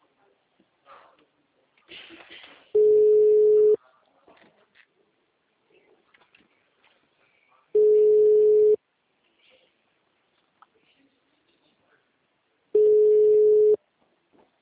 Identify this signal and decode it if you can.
Ringback tone (Europe, 425 Hz), with speech.